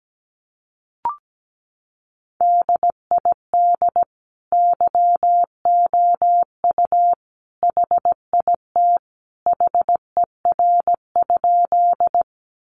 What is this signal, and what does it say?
Morse code, heard as a keyed tone, about 17 wpm, 700 Hz. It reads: DID YOU HIT HER?
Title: Sentences from Top 1200 Words 17wpm